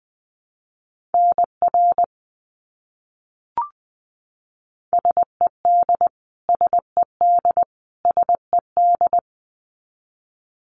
Morse code, heard as a keyed tone, about 20 wpm, 700 Hz. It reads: NR SED SED SED
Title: Abbreviations Repeated 3x - 20wpm